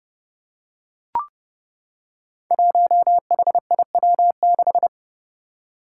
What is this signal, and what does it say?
Morse code, heard as a keyed tone, about 30 wpm, 700 Hz. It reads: 1HIW6
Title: Five Challenging Letter-Number Combinations 30wpm